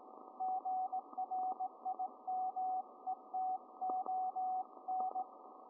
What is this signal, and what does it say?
Morse code, heard as a keyed tone, about 18 wpm, 750 Hz, with radio noise. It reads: GRIMETON